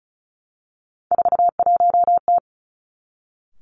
Morse code, heard as a keyed tone, about 36 wpm, 700 Hz. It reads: 41T
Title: Three Challenging Letter-Number Combinations 35wpm